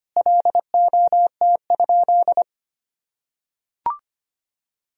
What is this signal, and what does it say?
Morse code, heard as a keyed tone, about 25 wpm, 700 Hz. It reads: LOT?
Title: Sentences from Top 1000 Words - Review Part 4 - 25wpm